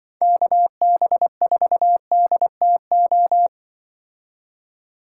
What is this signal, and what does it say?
Morse code, heard as a keyed tone, about 24 wpm, 700 Hz. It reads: KB4DTO
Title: Speed Racing - USA 2x3 Call Signs 20wpm